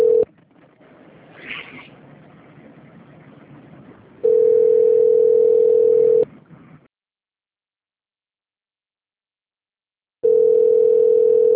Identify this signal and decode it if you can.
Ringback tone (North America), 440 + 480 Hz.